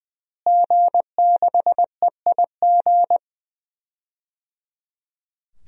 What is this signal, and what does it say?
Morse code, heard as a keyed tone, about 20 wpm, 700 Hz. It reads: G6EIG
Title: DXCC-100 Call Signs - 5 Characters - Repeated 3x - 20wpm